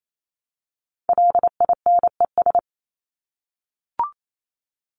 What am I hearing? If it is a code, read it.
Morse code, heard as a keyed tone, about 28 wpm, 700 Hz. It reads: LINES
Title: ES:S Suffix - Top 100 Words with Base Words 28wpm